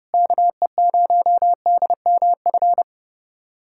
Morse code, heard as a keyed tone, about 30 wpm, 700 Hz. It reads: KE0DMF
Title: USA Callsign DE Callsign 30wpm